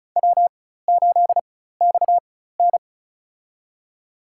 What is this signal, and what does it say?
Morse code, heard as a keyed tone, about 34 wpm, spaced as for 16 wpm, 700 Hz. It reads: W8XN
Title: USA Call Signs - 4 Characters - 4x Character Spacing - 35wpm